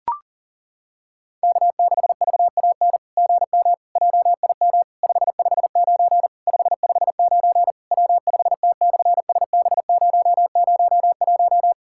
Morse code, heard as a keyed tone, about 40 wpm, 700 Hz. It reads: K6VAN GM JIM 559 559 W5T/SB001
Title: SOTA QSO Practice 40wpm